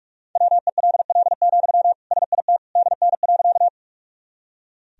Morse code, heard as a keyed tone, about 46 wpm, 690 Hz. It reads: WELL, SIT DN.
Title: Sentences from Top 500 Words and Abbreviations 45wpm